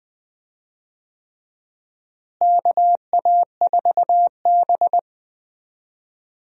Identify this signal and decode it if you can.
Morse code, heard as a keyed tone, about 20 wpm, 700 Hz. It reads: KA4B
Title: USA Call Signs - 4 Characters - Repeated 3x - 20wpm